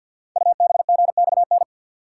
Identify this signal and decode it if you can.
Morse code, heard as a keyed tone, about 48 wpm, 700 Hz. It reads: UBGXN
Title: Code Groups - 5 Letters - 50wpm